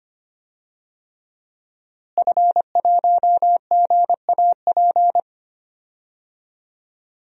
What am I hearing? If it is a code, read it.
Morse code, heard as a keyed tone, about 25 wpm, 700 Hz. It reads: F1GAP
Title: DXCC-100 Call Signs - 5 Characters - Repeated 3x - 25wpm